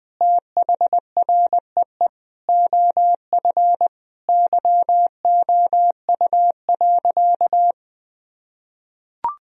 Morse code, heard as a keyed tone, about 21 wpm, 700 Hz. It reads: THREE OF YOU.
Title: Sentences from Top 1000 Words - Review Part 1 - 20wpm